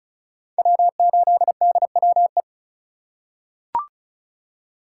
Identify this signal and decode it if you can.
Morse code, heard as a keyed tone, about 36 wpm, 700 Hz. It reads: W8DWE
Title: USA Call Signs - 5 Characters - 2x Character Spacing - 35wpm